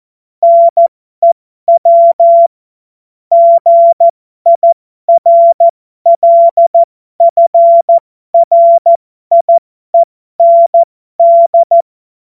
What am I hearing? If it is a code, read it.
Morse code, heard as a keyed tone, about 14 wpm, spaced as for 12 wpm, 680 Hz. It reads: NEW GIRLFRIEND
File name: Morsekurs / Witze